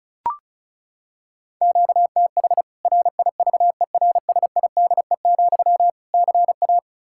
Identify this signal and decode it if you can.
Morse code, heard as a keyed tone, about 34 wpm, 700 Hz. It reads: QTH RIVERSIDE, CA
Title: Practice QSO - Exchange 1 - Beginning - 35wpm